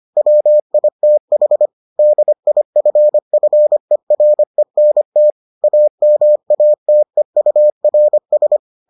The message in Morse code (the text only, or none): WITH DIFFERENT AMATEURS